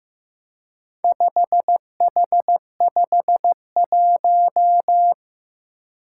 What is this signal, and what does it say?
Morse code, heard as a keyed tone, about 15 wpm, 700 Hz. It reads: 5H51
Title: Four Challenging Letter-Number Combinations 15wpm